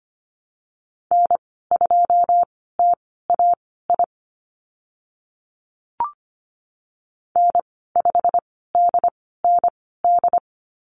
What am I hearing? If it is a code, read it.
Morse code, heard as a keyed tone, about 25 wpm, spaced as for 16 wpm, 700 Hz. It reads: N2TAI N5DND
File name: Lesson 15g - Number 5 - Call Signs 25wpm